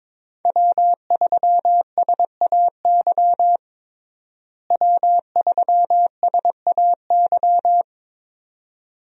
Morse code, heard as a keyed tone, about 22 wpm, 700 Hz. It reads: W3SAY W3SAY
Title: USA Call Signs - 5 Characters - Repeated 2x - 22wpm